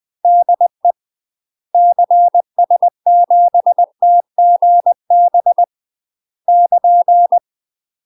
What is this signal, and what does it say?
Morse code, heard as a keyed tone, about 20 wpm, 700 Hz. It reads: DE CS7TGB <KN>
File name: UK MASTER QSOs TRAINING / 20WPM / UK MASTER QSOs 0010